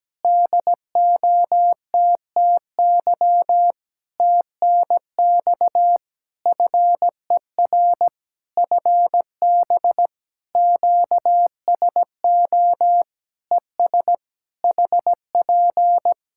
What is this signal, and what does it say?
Morse code, heard as a keyed tone, about 17 wpm, 700 Hz. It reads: DOTTY TNX FER FB QSO ES HP
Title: Practice QSO - Exchange 4 - Ending - 17wpm